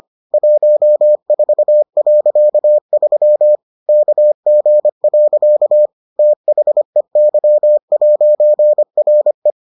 Morse code, heard as a keyed tone, about 25 wpm, 600 Hz. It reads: 14.3 KG. THEY'RE